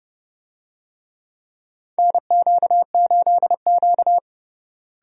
Morse code, heard as a keyed tone, about 30 wpm, 700 Hz. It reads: NQ8Q
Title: USA Call Signs - 4 Characters - Repeated 3x - 30wpm